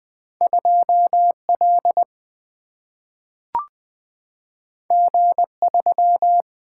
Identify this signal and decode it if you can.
Morse code, heard as a keyed tone, about 20 wpm, 700 Hz. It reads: 2L G3
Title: Two Letter-Number Combinations - 20wpm